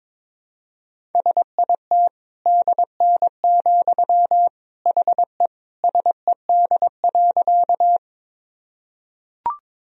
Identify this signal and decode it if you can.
Morse code, heard as a keyed tone, about 22 wpm, 700 Hz. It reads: SIT DN, HE SED.
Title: Sentences from Top 500 Words and Abbreviations 22wpm